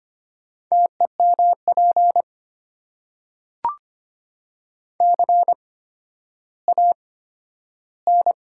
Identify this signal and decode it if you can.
Morse code, heard as a keyed tone, about 25 wpm, spaced as for 12 wpm, 700 Hz. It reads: TEMP CAN
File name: Top 90 QSO Elements - 8x Character Spacing - 25wpm